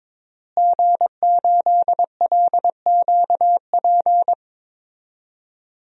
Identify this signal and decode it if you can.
Morse code, heard as a keyed tone, about 22 wpm, 700 Hz. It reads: G8LQP
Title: DXCC-100 Call Signs - 5 Characters - Repeated 2x - 22wpm